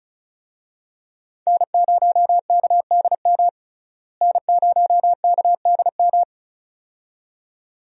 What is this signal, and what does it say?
Morse code, heard as a keyed tone, about 36 wpm, 700 Hz. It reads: N0KDM N0KDM
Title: USA Call Signs - 5 Characters - Repeated 2x - 35wpm